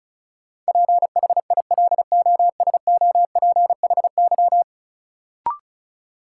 Morse code, heard as a keyed tone, about 36 wpm, 700 Hz. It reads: PHILOSOPHY
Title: Sentences from Top 900 Words - New Word Drill - 35wpm